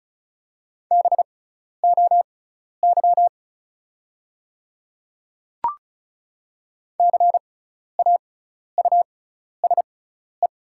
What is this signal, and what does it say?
Morse code, heard as a keyed tone, about 36 wpm, spaced as for 12 wpm, 700 Hz. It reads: BOY CAUSE
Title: 6x Character Spacing - Top 300 Words - 35wpm